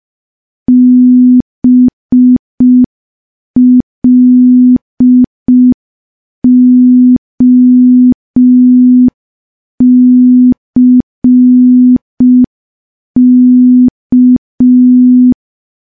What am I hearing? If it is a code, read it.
Morse code, heard as a keyed tone, about 5 wpm, 250 Hz. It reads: BLOCK
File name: morse / morse-training-05WPM